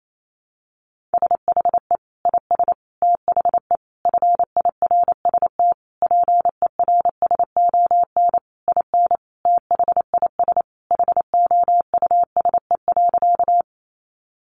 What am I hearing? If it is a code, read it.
Morse code, heard as a keyed tone, about 28 wpm, 700 Hz. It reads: SHE IS THE FIRST PERSON IN THIS HOUSE.
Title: Sentences from Top 400 Words 28wpm